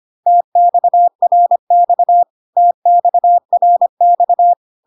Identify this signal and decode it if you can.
Morse code, heard as a keyed tone, about 25 wpm, 700 Hz. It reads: TXRX TXRX
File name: cw / CW common abbreviations alphabetical repeated 3 times 25 25wpm ik1hge